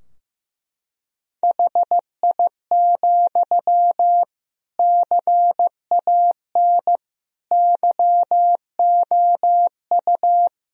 Morse code, heard as a keyed tone, about 15 wpm, 700 Hz. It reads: HI, CAN YOU
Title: Sentences from Top 1000 Words - Review Part 3 - 15wpm